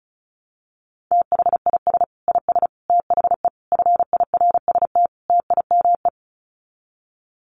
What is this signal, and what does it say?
Morse code, heard as a keyed tone, about 34 wpm, 700 Hz. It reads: THIS IS THE FIRST TIME
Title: Sets of 5 Words 35wpm